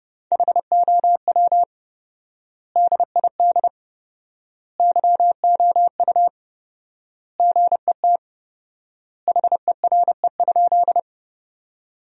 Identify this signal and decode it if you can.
Morse code, heard as a keyed tone, about 30 wpm, 700 Hz. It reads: HOW DID YOU GET HERE?
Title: 4x Word-Spacing - Sentences from Top 200 Words - 30wpm